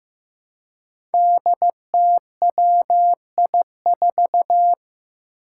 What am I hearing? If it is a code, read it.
Morse code, heard as a keyed tone, about 15 wpm, 700 Hz. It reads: DTWI4
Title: Five Challenging Letter-Number Combinations 15wpm